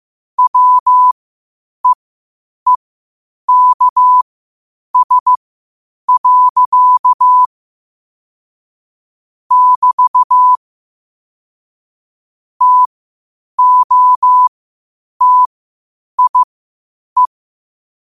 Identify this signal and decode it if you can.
Morse code, heard as a keyed tone, about 15 wpm, spaced as for 8 wpm, 1000 Hz. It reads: WEEKS. <BT> TOTIE
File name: qotdfiles / 8wpm